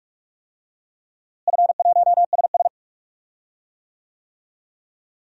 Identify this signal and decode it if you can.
Morse code, heard as a keyed tone, about 41 wpm, 700 Hz. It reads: F1SS
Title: DXCC-100 Call Signs - 4 Characters - Repeated 2x - 45wpm